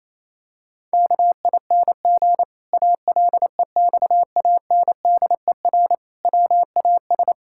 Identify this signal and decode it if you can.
Morse code, heard as a keyed tone, about 28 wpm, 700 Hz. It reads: KING ALEXANDER WAS
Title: Pangram Challenge v2 - 28wpm